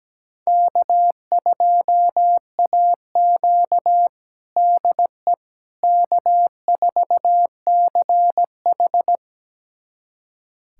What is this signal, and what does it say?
Morse code, heard as a keyed tone, about 17 wpm, 700 Hz. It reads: K2AQ DE K4CH
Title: USA Callsign DE Callsign - 4 Characters - 17wpm